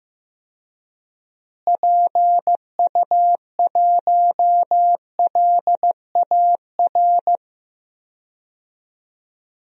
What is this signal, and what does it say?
Morse code, heard as a keyed tone, about 15 wpm, 700 Hz. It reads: PU1LAR
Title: DXCC-100 Call Signs - 6 Characters - Repeated 3x - 15wpm